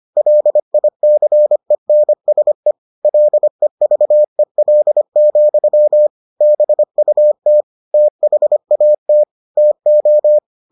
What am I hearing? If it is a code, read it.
Morse code, heard as a keyed tone, about 25 wpm, 600 Hz. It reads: LICENSE LEVEL, BUT THAT TO